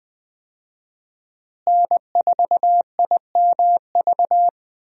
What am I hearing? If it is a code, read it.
Morse code, heard as a keyed tone, about 21 wpm, 700 Hz. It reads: N4IMV
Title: USA Call Signs - 5 Characters - Repeated 2x - 20wpm